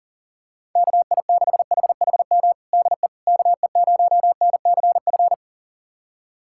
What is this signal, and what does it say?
Morse code, heard as a keyed tone, about 40 wpm, 700 Hz. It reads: KI6HHM DE KE0NCF
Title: USA Callsign DE Callsign 40wpm